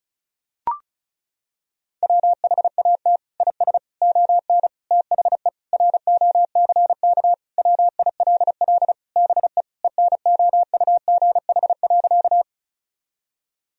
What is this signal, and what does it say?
Morse code, heard as a keyed tone, about 36 wpm, 700 Hz. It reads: WHAT IS ON THE ROCK WILL BE ENOUGH.
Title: Sentences from Top 1000 Words - Review Part 2 - 35wpm